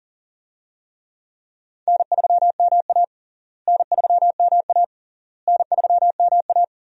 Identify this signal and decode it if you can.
Morse code, heard as a keyed tone, about 40 wpm, 700 Hz. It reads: N3MA N3MA N3MA
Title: USA Call Signs - 4 Characters - Repeated 3x - 40wpm